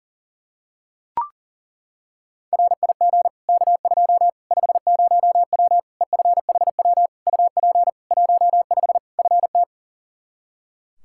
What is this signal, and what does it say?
Morse code, heard as a keyed tone, about 40 wpm, 700 Hz. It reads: RIG K2 50W EFHW UP 15 FT
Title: Practice QSO - Exchange 2 - Info - 40wpm